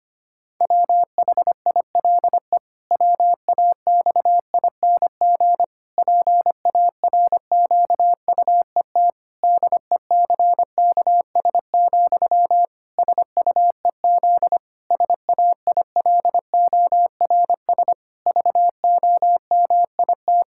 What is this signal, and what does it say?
Morse code, heard as a keyed tone, about 25 wpm, 700 Hz. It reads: WHILE WAXING PARQUET DECKS, SUEZ SAILORS VOMIT
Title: Pangram Challenge v2 - 25wpm